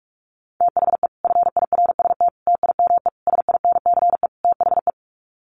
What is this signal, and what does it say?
Morse code, heard as a keyed tone, about 44 wpm, 700 Hz. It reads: THE FIRST TIME SINCE THE
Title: Sets of 5 Words Encore 45wpm